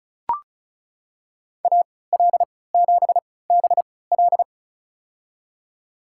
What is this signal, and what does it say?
Morse code, heard as a keyed tone, about 35 wpm, spaced as for 20 wpm, 700 Hz. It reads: AL7BL
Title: USA Call Signs - 5 Characters - 3x Character Spacing - 35wpm